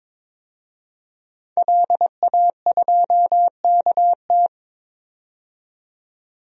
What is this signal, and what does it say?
Morse code, heard as a keyed tone, about 22 wpm, 700 Hz. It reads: LA2KT
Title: DXCC-100 Call Signs - 5 Characters - 22wpm